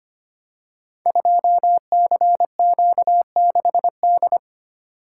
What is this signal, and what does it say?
Morse code, heard as a keyed tone, about 25 wpm, 700 Hz. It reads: 2CQ6D
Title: Code Groups - 5 Letter:Numbers - 25wpm